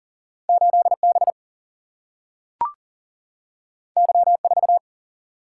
Morse code, heard as a keyed tone, about 40 wpm, 700 Hz. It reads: 8B Y4
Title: Single-Letter Single-Number 40wpm